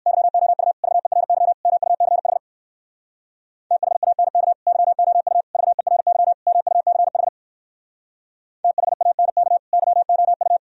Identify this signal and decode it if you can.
Morse code, heard as a keyed tone, about 58 wpm, 700 Hz. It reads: YOU VERY MUCH THANK YOU VERY MUCH THANK YOU